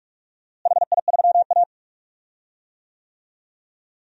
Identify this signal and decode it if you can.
Morse code, heard as a keyed tone, about 47 wpm, 700 Hz. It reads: HI3A
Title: DXCC-100 Call Signs - 4 Characters - Repeated 2x - 45wpm